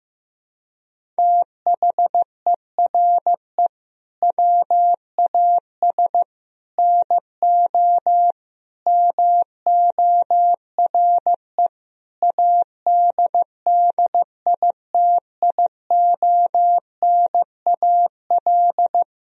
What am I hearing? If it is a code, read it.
Morse code, heard as a keyed tone, about 14 wpm, 700 Hz. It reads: THERE WAS NO MORE ADDITIONAL
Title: Sentences from Top 1000 Words - Review Part 4 - 15wpm